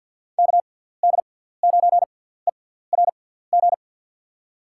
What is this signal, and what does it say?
Morse code, heard as a keyed tone, about 51 wpm, spaced as for 17 wpm, 700 Hz. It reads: KD9ERG
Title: USA Call Signs - 6 Characters - 6x Character Spacing - 50wpm